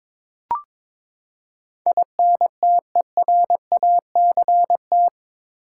Morse code, heard as a keyed tone, about 22 wpm, 700 Hz. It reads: INTERACT
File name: practice / 22wpm / Eight Letter Words - 22wpm